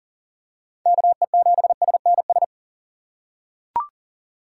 Morse code, heard as a keyed tone, about 40 wpm, 700 Hz. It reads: KE7SNS
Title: USA Call Signs - 6 Characters - 3x Character Spacing - 40wpm